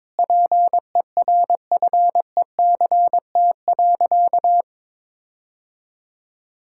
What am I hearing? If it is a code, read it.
Morse code, heard as a keyed tone, about 22 wpm, 700 Hz. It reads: PERFECT.
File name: Sentences with 9-Word Phrases - 22wpm